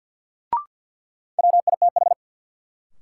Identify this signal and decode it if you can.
Morse code, heard as a keyed tone, about 51 wpm, 700 Hz. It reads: WITH
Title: Sets of 2 Words - New Word Drill - 50wpm